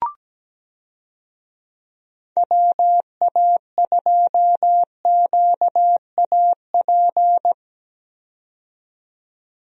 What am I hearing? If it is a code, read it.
Morse code, heard as a keyed tone, about 17 wpm, 700 Hz. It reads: WA2QAP
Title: USA Call Signs - 6 Characters - 17wpm